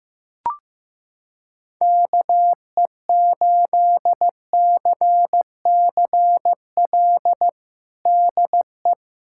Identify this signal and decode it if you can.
Morse code, heard as a keyed tone, about 15 wpm, 700 Hz. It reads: KE8CCL DE
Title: USA Callsign DE Callsign - 6 Characters - 15wpm